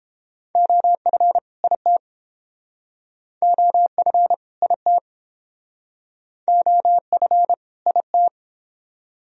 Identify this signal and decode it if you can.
Morse code, heard as a keyed tone, about 31 wpm, 700 Hz. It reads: OF IT OF IT OF IT